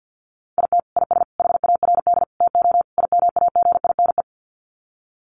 Morse code, heard as a keyed tone, about 50 wpm, 700 Hz. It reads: IT IS HARD TO IMAGINE
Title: Sets of 5 Words Encore2 50wpm